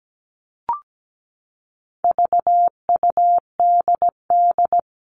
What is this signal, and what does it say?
Morse code, heard as a keyed tone, about 17 wpm, 700 Hz. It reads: VUDD
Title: All Cut Numbers - 4-Digit 17wpm